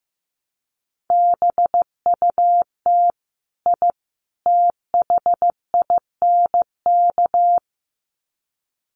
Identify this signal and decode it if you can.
Morse code, heard as a keyed tone, about 15 wpm, 700 Hz. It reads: BUT I THINK
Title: Sets of 3 Words 15wpm